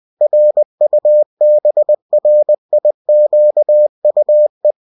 Morse code, heard as a keyed tone, about 20 wpm, 600 Hz. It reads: RUBRIQUE